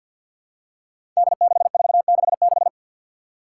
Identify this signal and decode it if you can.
Morse code, heard as a keyed tone, about 46 wpm, 700 Hz. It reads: D6466
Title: Five Challenging Letter-Number Combinations 50wpm